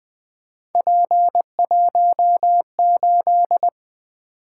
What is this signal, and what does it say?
Morse code, heard as a keyed tone, about 20 wpm, 700 Hz. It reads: P18
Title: Three Letter-Number Combinations - 20wpm